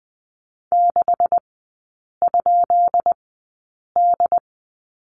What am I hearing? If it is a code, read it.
Morse code, heard as a keyed tone, about 20 wpm, spaced as for 8 wpm, 690 Hz. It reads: 6?D
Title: 133 Lesson 22e - Question Mark - Three Characters 20wpm